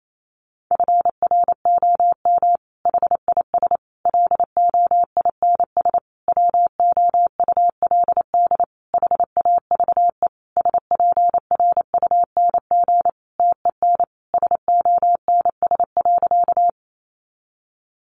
Morse code, heard as a keyed tone, about 28 wpm, 700 Hz. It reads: FROM HIS LOINS WOULD HAVE SPRUNG TEN SONS.
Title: A Sound Of Thunder 28wpm